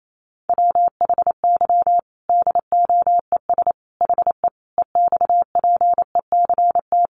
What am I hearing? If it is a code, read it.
Morse code, heard as a keyed tone, about 28 wpm, 700 Hz. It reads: WHY DOES HE EXPECT
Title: Sentences from Top 1100 Words 28wpm